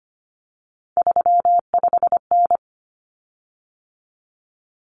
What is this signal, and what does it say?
Morse code, heard as a keyed tone, about 25 wpm, 700 Hz. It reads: 35N
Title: Common Cut Numbers - 3-Digit 25wpm